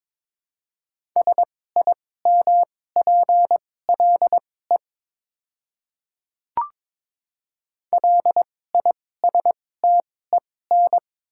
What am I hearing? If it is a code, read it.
Morse code, heard as a keyed tone, about 22 wpm, spaced as for 16 wpm, 700 Hz. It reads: SIMPLE LISTEN
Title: 2x Character Spacing - Top 500 Words - 22wpm